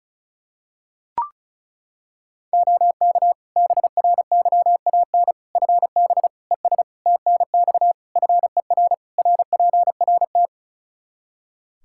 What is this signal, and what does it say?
Morse code, heard as a keyed tone, about 36 wpm, 700 Hz. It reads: OK BRYAN FB ES TNX FER RPRT
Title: Practice QSO - Exchange 2 - Info - 35wpm